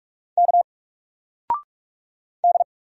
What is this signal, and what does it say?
Morse code, heard as a keyed tone, about 44 wpm, 700 Hz. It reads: K D
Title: Single Character (Letter:Number:Punctuation:Prosign) 45wpm